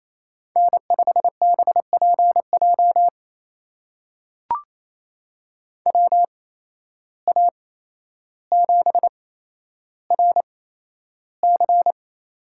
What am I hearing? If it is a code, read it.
Morse code, heard as a keyed tone, about 28 wpm, spaced as for 12 wpm, 700 Hz. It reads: N5BPJ WA7RC
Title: USA Call Signs - 5 Characters - 8x Character Spacing - 28wpm